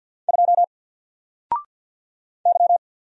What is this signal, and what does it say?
Morse code, heard as a keyed tone, about 50 wpm, 700 Hz. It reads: 2 Y
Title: Single Letter-Number 50wpm